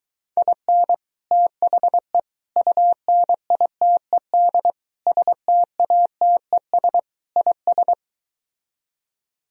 Morse code, heard as a keyed tone, about 24 wpm, 700 Hz. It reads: IN THE UNITED STATES IS